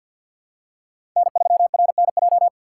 Morse code, heard as a keyed tone, about 51 wpm, 700 Hz. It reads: N3RNJ
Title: USA Call Signs - 5 Characters - 2x Character Spacing - 50wpm